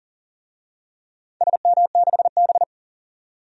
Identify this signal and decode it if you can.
Morse code, heard as a keyed tone, about 37 wpm, 700 Hz. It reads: SM6B
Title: DXCC-100 Call Signs - 4 Characters - 40wpm